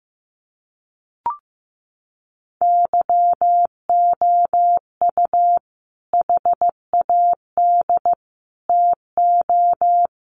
Morse code, heard as a keyed tone, about 15 wpm, 700 Hz. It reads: YOU HAD TO
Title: Sentences from Top 600 Words 15wpm